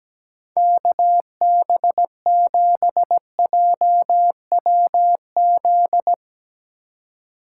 Morse code, heard as a keyed tone, about 17 wpm, 700 Hz. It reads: KB7JWZ
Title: USA Call Signs - 6 Characters - 2x Character Spacing - 17wpm